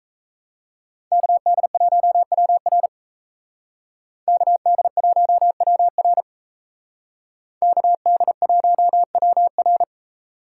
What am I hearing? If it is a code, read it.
Morse code, heard as a keyed tone, about 36 wpm, 700 Hz. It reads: KD1WR KD1WR KD1WR